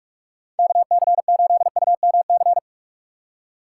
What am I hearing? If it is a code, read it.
Morse code, heard as a keyed tone, about 46 wpm, 700 Hz. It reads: KC8UMC
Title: USA Call Signs - 6 Characters - 6x Character Spacing - 45wpm